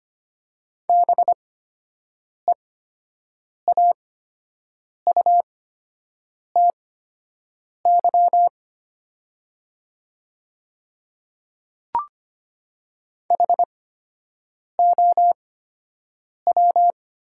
Morse code, heard as a keyed tone, about 25 wpm, spaced as for 7 wpm, 700 Hz. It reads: BEAUTY HOW
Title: 8x Character Spacing - Top 500 Words - 25wpm